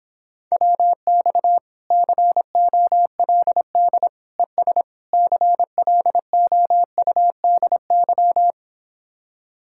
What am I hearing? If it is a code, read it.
Morse code, heard as a keyed tone, about 26 wpm, 700 Hz. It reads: WX COLD ES CLOUDY